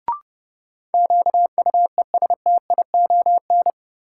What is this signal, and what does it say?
Morse code, heard as a keyed tone, about 30 wpm, 700 Hz. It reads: QUESTION
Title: Sentences from Top 400 Words - New Word Drill - 30wpm